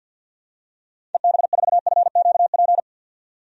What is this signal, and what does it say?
Morse code, heard as a keyed tone, about 53 wpm, 700 Hz. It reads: EB4FQP
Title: DXCC-100 Call Signs - 6 Characters - 50wpm